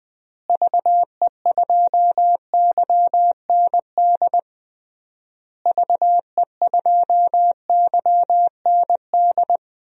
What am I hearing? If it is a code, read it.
Morse code, heard as a keyed tone, about 20 wpm, 700 Hz. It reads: VE2YND VE2YND
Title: DXCC-100 Call Signs - 6 Characters - Repeated 3x - 20wpm